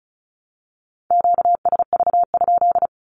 Morse code, heard as a keyed tone, about 34 wpm, 700 Hz. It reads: QSV?
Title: Q-Codes Repeated 3x 35wpm